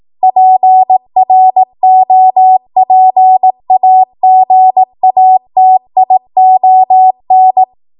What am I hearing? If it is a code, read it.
Morse code, heard as a keyed tone, about 18 wpm, 750 Hz. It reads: PROPAGATION
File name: Morse / 18WPM